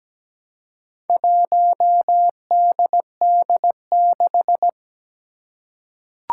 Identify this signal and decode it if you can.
Morse code, heard as a keyed tone, about 17 wpm, 700 Hz. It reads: 1DD6
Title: Four Challenging Letter-Number Combinations 17wpm